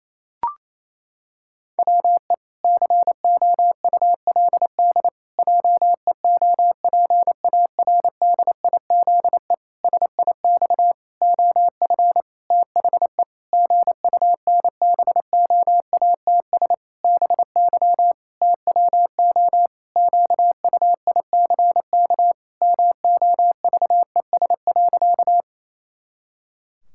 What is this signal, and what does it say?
Morse code, heard as a keyed tone, about 28 wpm, 700 Hz. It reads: WE COULD JEOPARDIZE SIX OF THE GUNBOATS BY TWO QUICK MOVES.
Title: Pangram Challenge v2 - 28wpm